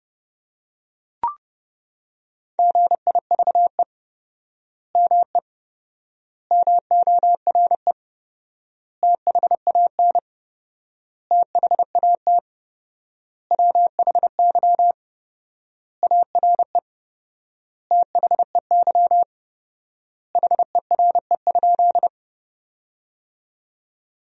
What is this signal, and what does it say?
Morse code, heard as a keyed tone, about 30 wpm, 700 Hz. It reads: GIVE ME MORE THAN THAT WHY ARE THEY HERE?
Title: 4x Word-Spacing - Sentences from Top 200 Words - 30wpm